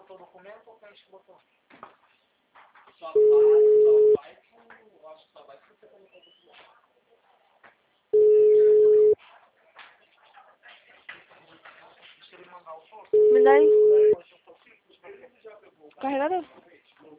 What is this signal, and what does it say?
Ringback tone (Europe, 425 Hz), with speech.
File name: out- -2008- - - .2